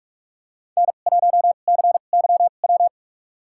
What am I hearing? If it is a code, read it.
Morse code, heard as a keyed tone, about 46 wpm, spaced as for 33 wpm, 700 Hz. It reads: N1CYW
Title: USA Call Signs - 5 Characters - 2x Character Spacing - 45wpm